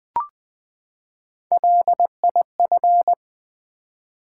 Morse code, heard as a keyed tone, about 20 wpm, 700 Hz. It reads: LIF
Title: Build A Word - 3 Letters At A Time - 8-Letter Words - 20wpm